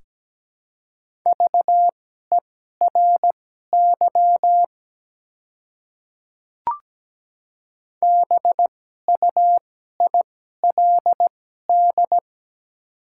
Morse code, heard as a keyed tone, about 17 wpm, spaced as for 12 wpm, 700 Hz. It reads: VERY BUILD
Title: 2x Character Spacing - Top 200 Words - 17wpm